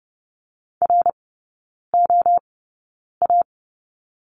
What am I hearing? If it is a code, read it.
Morse code, heard as a keyed tone, about 30 wpm, spaced as for 9 wpm, 690 Hz. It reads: ROA
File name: Lesson 17e - Letter u - Three Characters 30wpm